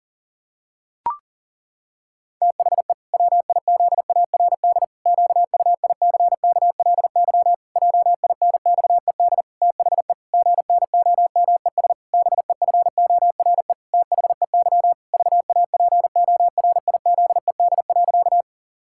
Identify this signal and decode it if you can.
Morse code, heard as a keyed tone, about 37 wpm, 700 Hz. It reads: THE WIZARD QUICKLY JINXED THE GNOMES BEFORE THEY VAPORIZED.
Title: Pangram Challenge v2 - 40wpm